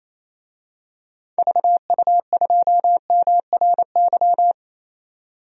Morse code, heard as a keyed tone, about 27 wpm, 700 Hz. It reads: VU2MRY
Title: DXCC-100 Call Signs - 6 Characters - 28wpm